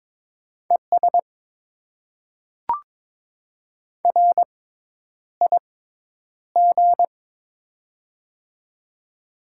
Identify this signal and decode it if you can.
Morse code, heard as a keyed tone, about 22 wpm, spaced as for 10 wpm, 690 Hz. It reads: ES RIG
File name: Top 20 QSO Elements - 6x Character Spacing - 22wpm